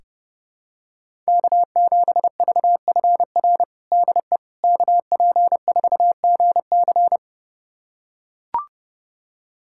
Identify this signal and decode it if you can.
Morse code, heard as a keyed tone, about 30 wpm, 700 Hz. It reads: K7VFR DE KP4GC
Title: USA Callsign DE Callsign - 5 Characters - 30wpm